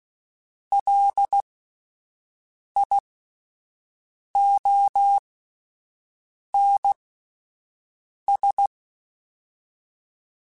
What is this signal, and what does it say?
Morse code, heard as a keyed tone, about 16 wpm, spaced as for 5 wpm, 800 Hz. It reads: LIONS
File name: Misc / Misc-mp3s / Word Pairs @5